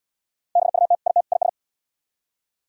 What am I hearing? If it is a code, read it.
Morse code, heard as a keyed tone, about 76 wpm, 700 Hz. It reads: BUT IT IS